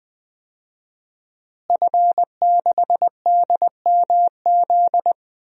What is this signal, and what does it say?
Morse code, heard as a keyed tone, about 20 wpm, 700 Hz. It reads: F6DMZ
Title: DXCC-100 Call Signs - 5 Characters - Repeated 2x - 20wpm